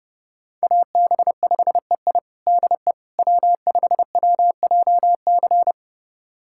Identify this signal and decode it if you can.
Morse code, heard as a keyed tone, about 30 wpm, 700 Hz. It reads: AB5EI DE W5WJC
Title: USA Callsign DE Callsign - 5 Characters - 30wpm